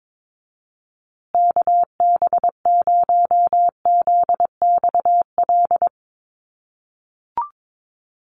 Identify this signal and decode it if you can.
Morse code, heard as a keyed tone, about 22 wpm, 700 Hz. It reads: KB0ZXL
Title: USA Call Signs Repeated 2x 22wpm